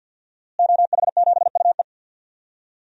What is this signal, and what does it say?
Morse code, heard as a keyed tone, about 51 wpm, 700 Hz. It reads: OH7UE
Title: DXCC-100 Call Signs - 5 Characters - Repeated 3x - 50wpm